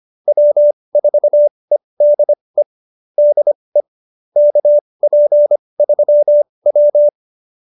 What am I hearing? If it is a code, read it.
Morse code, heard as a keyed tone, about 25 wpm, spaced as for 20 wpm, 590 Hz. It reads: W4EDE DE KP3W
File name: QSO317 20